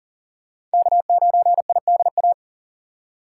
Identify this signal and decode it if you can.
Morse code, heard as a keyed tone, about 40 wpm, 700 Hz. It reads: K9IDA
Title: USA Call Signs - 5 Characters - 2x Character Spacing - 40wpm